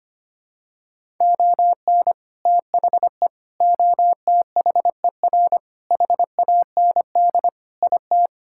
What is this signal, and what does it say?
Morse code, heard as a keyed tone, about 25 wpm, 700 Hz. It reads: ON THE OTHER HAND IT
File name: practice / 25wpm / Sentences with 3-Word Phrases - 25wpm